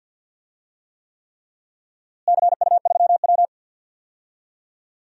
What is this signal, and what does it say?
Morse code, heard as a keyed tone, about 47 wpm, 700 Hz. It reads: CU3W
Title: DXCC-100 Call Signs - 4 Characters - Repeated 3x - 50wpm